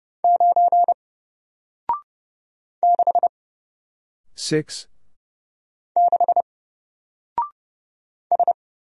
Morse code, heard as a keyed tone, about 30 wpm, 700 Hz. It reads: 9 6 6 S
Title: Single Letter-Number 30wpm